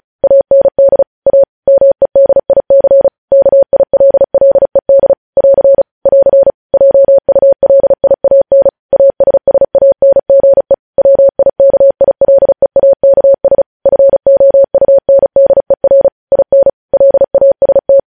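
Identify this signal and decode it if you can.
Morse code, heard as a keyed tone, about 36 wpm, 550 Hz. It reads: AND A MEDIC KILLED <AR> <AR> JULIAN ASSANGE WIKILEAKS FOUNDER IN LAST
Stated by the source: Morse code news headlines sent at 35 words per minute.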